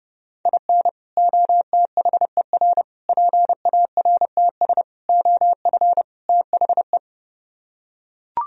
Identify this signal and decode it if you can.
Morse code, heard as a keyed tone, about 30 wpm, 700 Hz. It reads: IN OTHER PARTS OF THE